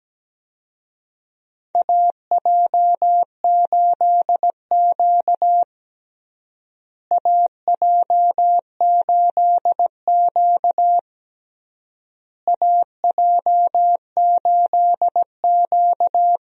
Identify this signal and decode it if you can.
Morse code, heard as a keyed tone, about 17 wpm, 700 Hz. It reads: AJ8Q AJ8Q AJ8Q
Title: USA Call Signs - 4 Characters - Repeated 3x - 17wpm